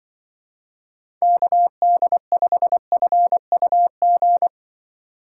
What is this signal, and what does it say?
Morse code, heard as a keyed tone, about 23 wpm, 700 Hz. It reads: KD5FUG
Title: Speed Racing - USA Call Signs 20wpm